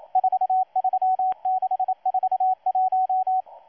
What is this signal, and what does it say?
Morse code, heard as a keyed tone, about 27 wpm, 750 Hz, with radio noise. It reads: 43641
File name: M12-2016-07-07-1310utc-13926khz-error-restart